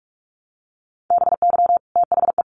Morse code, heard as a keyed tone, about 44 wpm, 700 Hz. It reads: BY THE
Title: Sets of 3 Words 45wpm